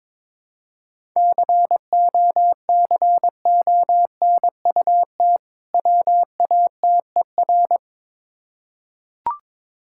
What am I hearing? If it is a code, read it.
Morse code, heard as a keyed tone, about 22 wpm, 690 Hz. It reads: COCONUT WATER
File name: Difficult Words with Hints - 22wpm